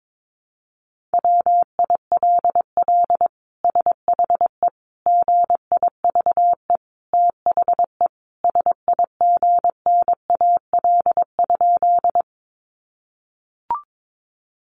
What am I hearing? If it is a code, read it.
Morse code, heard as a keyed tone, about 22 wpm, 700 Hz. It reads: WILL SHE GIVE THE SIGNAL?
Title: Sentences from Top 1400 Words 22wpm